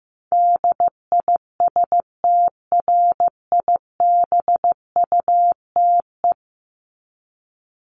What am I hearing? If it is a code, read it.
Morse code, heard as a keyed tone, about 15 wpm, 700 Hz. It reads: DISTRIBUTE
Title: Top 1501-1750 Words 15wpm